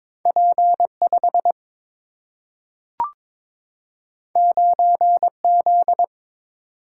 Morse code, heard as a keyed tone, about 22 wpm, 700 Hz. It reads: P5 9Z
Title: Single-Letter Single-Number 22wpm